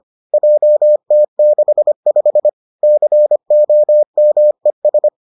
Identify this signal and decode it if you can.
Morse code, heard as a keyed tone, about 25 wpm, 600 Hz. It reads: JT65 COMES